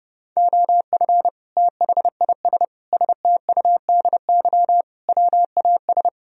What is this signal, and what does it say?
Morse code, heard as a keyed tone, about 36 wpm, 700 Hz. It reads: OF THIS STUDY WAS
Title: Speed-Racing - Sets of 5 Words 30wpm